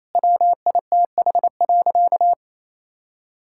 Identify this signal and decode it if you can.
Morse code, heard as a keyed tone, about 28 wpm, 700 Hz. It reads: WITH.
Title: Sentences from Top 1000 Words - Review Part 5 - 28wpm